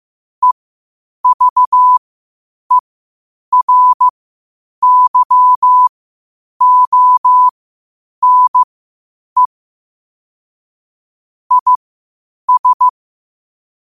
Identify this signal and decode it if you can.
Morse code, heard as a keyed tone, about 15 wpm, spaced as for 8 wpm, 1000 Hz. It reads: EVERYONE IS
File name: qotdfiles / 8wpm